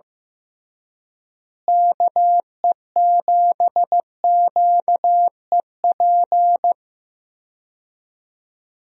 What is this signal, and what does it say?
Morse code, heard as a keyed tone, about 15 wpm, 700 Hz. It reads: KE7QEP
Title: USA Call Signs - 6 Characters - Repeated 3x - 15wpm